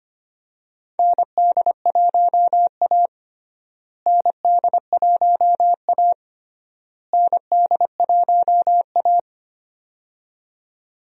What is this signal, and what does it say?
Morse code, heard as a keyed tone, about 25 wpm, 700 Hz. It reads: ND1A ND1A ND1A
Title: USA Call Signs - 4 Characters - Repeated 3x - 25wpm